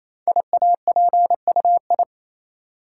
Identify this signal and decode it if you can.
Morse code, heard as a keyed tone, about 27 wpm, 700 Hz. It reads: IAPUI
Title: Code Groups - 5 Letters - 28wpm